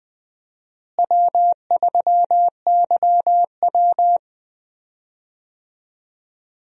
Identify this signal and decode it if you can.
Morse code, heard as a keyed tone, about 20 wpm, 700 Hz. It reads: W3YW
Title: USA Call Signs - 4 Characters - Repeated 3x - 20wpm